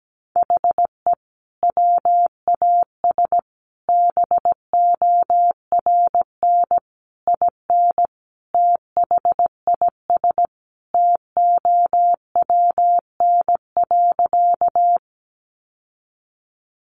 Morse code, heard as a keyed tone, about 17 wpm, 700 Hz. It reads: HE WAS BORN IN THIS TOWN.
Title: Sentences from Top 1000 Words 17wpm